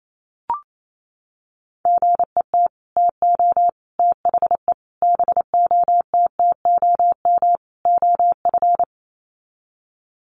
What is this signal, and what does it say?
Morse code, heard as a keyed tone, about 28 wpm, 700 Hz. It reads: GET TO THE BOTTOM OF
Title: Sets of 5 Words Encore 28wpm